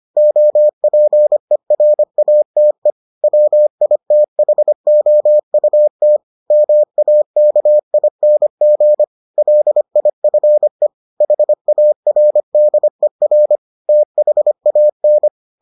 Morse code, heard as a keyed tone, about 25 wpm, 600 Hz. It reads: OPERATE WITHOUT MAKING LIFE HARDER THAN